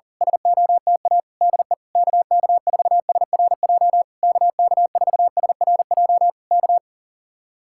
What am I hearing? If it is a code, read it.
Morse code, heard as a keyed tone, about 37 wpm, 700 Hz. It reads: SOTA DE KK4SRJ KK4SRJ K
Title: Speed-Racing Common QSO Phrases Encore - 40wpm